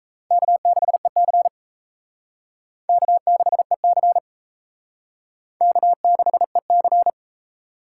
Morse code, heard as a keyed tone, about 34 wpm, 700 Hz. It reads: K6EC K6EC K6EC
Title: Speed Racing - USA Call Signs 28wpm